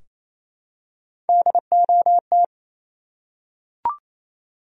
Morse code, heard as a keyed tone, about 28 wpm, 700 Hz. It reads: DOT
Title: USA Common Names - Max 3 Letters - 28wpm